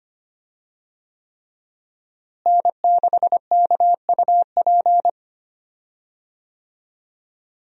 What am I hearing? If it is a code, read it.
Morse code, heard as a keyed tone, about 25 wpm, 700 Hz. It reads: N6KUP
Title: USA Call Signs - 5 Characters - Repeated 3x - 25wpm